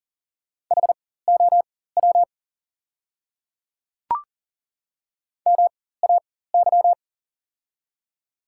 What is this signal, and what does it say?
Morse code, heard as a keyed tone, about 40 wpm, spaced as for 19 wpm, 700 Hz. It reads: HOW MAY
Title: 4x Character Spacing - Top 100 Words - 40wpm